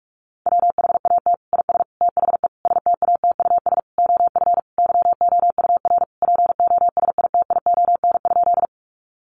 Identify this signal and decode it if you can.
Morse code, heard as a keyed tone, about 44 wpm, 700 Hz. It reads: WHAT IS THE STATUS OF YOUR POSITION?
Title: Sentences from Top 1400 Words 45wpm